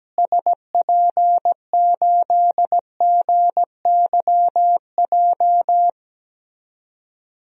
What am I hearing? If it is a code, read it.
Morse code, heard as a keyed tone, about 17 wpm, 700 Hz. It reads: SP8GYJ
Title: DXCC-100 Call Signs - 6 Characters - Repeated 2x - 17wpm